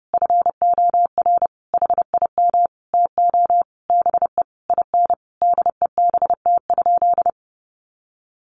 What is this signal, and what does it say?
Morse code, heard as a keyed tone, about 30 wpm, 700 Hz. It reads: FOR HIM TO BE IN DEBT?
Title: Sentences from Top 1300 Words 30wpm